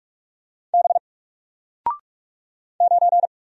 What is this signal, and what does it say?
Morse code, heard as a keyed tone, about 44 wpm, 700 Hz. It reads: B 9
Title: Single Letter-Number Alt 45wpm